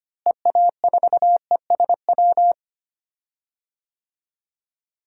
Morse code, heard as a keyed tone, about 25 wpm, 700 Hz. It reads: EA4ESW
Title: DXCC-100 Call Signs - 6 Characters - Repeated 2x - 25wpm